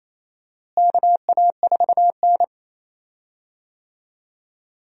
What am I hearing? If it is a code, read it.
Morse code, heard as a keyed tone, about 28 wpm, 700 Hz. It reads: KA4N
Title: USA Call Signs - 4 Characters - Repeated 2x - 28wpm